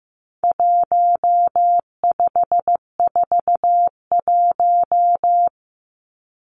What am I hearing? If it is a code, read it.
Morse code, heard as a keyed tone, about 15 wpm, 700 Hz. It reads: 1541
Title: Numbers - 4-Digit 15wpm